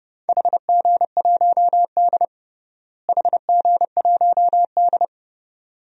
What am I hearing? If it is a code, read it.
Morse code, heard as a keyed tone, about 32 wpm, 700 Hz. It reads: HG1D HG1D
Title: DXCC-100 Call Signs - 4 Characters - Repeated 2x - 30wpm